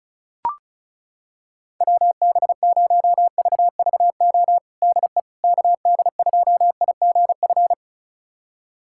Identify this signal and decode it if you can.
Morse code, heard as a keyed tone, about 36 wpm, 700 Hz. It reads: WB0VVO DE KD2IGF
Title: USA Callsign DE Callsign - 6 Characters - 35wpm